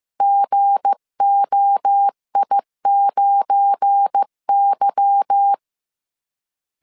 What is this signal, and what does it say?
Morse code, heard as a keyed tone, about 15 wpm, 800 Hz. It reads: GOI9Y
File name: CW / DET / 1 9 1500C S B 14wpm 3302